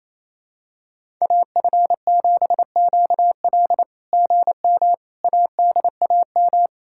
Morse code, heard as a keyed tone, about 28 wpm, 660 Hz. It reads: AF7QL GM ADAM
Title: SOTA QSO Practice 28wpm